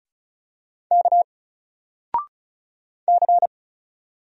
Morse code, heard as a keyed tone, about 35 wpm, 700 Hz. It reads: K C
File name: Single-Letter-Number-35wpm